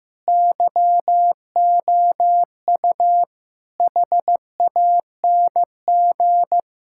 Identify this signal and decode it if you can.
Morse code, heard as a keyed tone, about 15 wpm, 700 Hz. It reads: YOU HANG
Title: Sentences from Top 1600 Words 15wpm